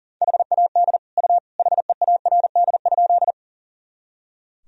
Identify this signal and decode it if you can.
Morse code, heard as a keyed tone, about 40 wpm, 700 Hz. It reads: HAD U HEARD?
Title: Sentences from Top 400 Words and Abbreviations 40wpm